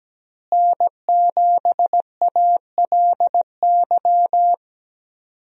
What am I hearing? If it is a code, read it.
Morse code, heard as a keyed tone, about 17 wpm, 700 Hz. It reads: N7ALY
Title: USA Call Signs - 5 Characters - Repeated 2x - 17wpm